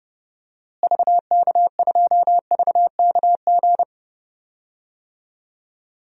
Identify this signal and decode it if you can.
Morse code, heard as a keyed tone, about 30 wpm, 700 Hz. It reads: VK2VKG
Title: DXCC-100 Call Signs - 6 Characters - Repeated 2x - 30wpm